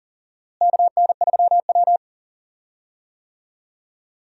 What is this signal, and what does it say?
Morse code, heard as a keyed tone, about 37 wpm, 700 Hz. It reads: KN3W
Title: USA Call Signs - 4 Characters - Repeated 2x - 40wpm